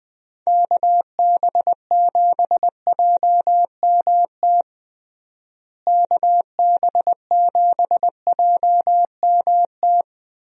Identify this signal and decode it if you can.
Morse code, heard as a keyed tone, about 20 wpm, 700 Hz. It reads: KB7JMT KB7JMT
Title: USA Call Signs - 6 Characters - Repeated 3x - 20wpm